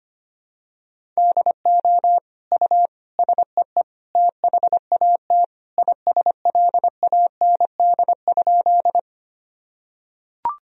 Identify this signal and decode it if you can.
Morse code, heard as a keyed tone, about 25 wpm, 700 Hz. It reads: DO U SEE THAT ISLAND?
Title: Sentences from Top 500 Words and Abbreviations 25wpm